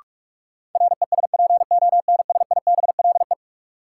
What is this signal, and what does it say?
Morse code, heard as a keyed tone, about 42 wpm, 700 Hz. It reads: RESPONSIBLE
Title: Sentences from Top 1000 Words - Review Part 7 - New Word Drill - 45wpm